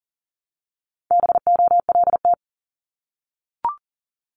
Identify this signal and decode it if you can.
Morse code, heard as a keyed tone, about 41 wpm, 700 Hz. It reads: BOLT
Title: Multi-letter Phonemes Repeated 3x 40wpm